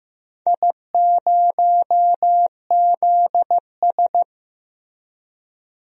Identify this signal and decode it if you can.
Morse code, heard as a keyed tone, about 15 wpm, 700 Hz. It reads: I0ZS
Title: DXCC-100 Call Signs - 4 Characters - Repeated 3x - 15wpm